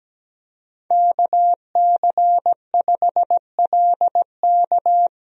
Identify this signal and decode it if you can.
Morse code, heard as a keyed tone, about 17 wpm, 700 Hz. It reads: KC5LK
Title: USA Call Signs - 5 Characters - 17wpm